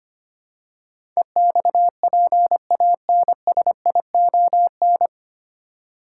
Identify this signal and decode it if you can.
Morse code, heard as a keyed tone, about 25 wpm, 700 Hz. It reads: EXPANSION
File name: practice / 25wpm / 9 Letter Words - First 3 Letters Given - 25wpm